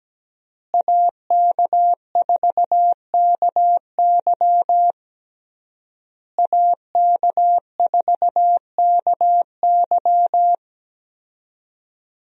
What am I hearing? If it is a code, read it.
Morse code, heard as a keyed tone, about 17 wpm, 700 Hz. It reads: AK4KY AK4KY
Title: USA Call Signs - 5 Characters - Repeated 3x - 17wpm